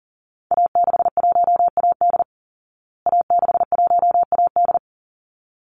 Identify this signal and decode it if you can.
Morse code, heard as a keyed tone, about 40 wpm, 700 Hz. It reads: A61AD A61AD
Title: Call Signs Repeated 3x - 40wpm